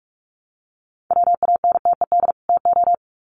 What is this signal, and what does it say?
Morse code, heard as a keyed tone, about 44 wpm, 700 Hz. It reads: WANTED TO
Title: Sets of 2 Words 45wpm